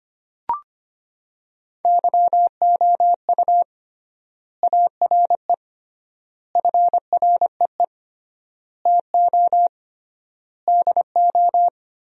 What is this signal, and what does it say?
Morse code, heard as a keyed tone, about 25 wpm, 700 Hz. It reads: YOU ARE FREE TO DO
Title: 3x Word-Spacing - Sentences from Top 500 Words - 25wpm